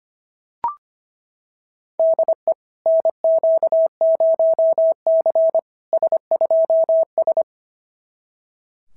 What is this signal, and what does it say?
Morse code, heard as a keyed tone, about 25 wpm, 690 Hz. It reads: DE NQ0C S2S
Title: SOTA QSO Practice 25wpm